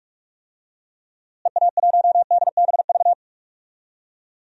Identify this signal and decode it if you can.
Morse code, heard as a keyed tone, about 46 wpm, 700 Hz. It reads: EA1DBV
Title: DXCC-100 Call Signs - 6 Characters - Repeated 3x - 45wpm